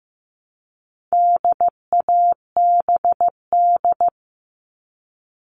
Morse code, heard as a keyed tone, about 15 wpm, 700 Hz. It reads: DABD
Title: All Cut Numbers - 4-Digit 15wpm